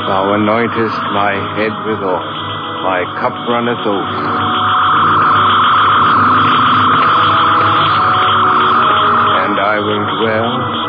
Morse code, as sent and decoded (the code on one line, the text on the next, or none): none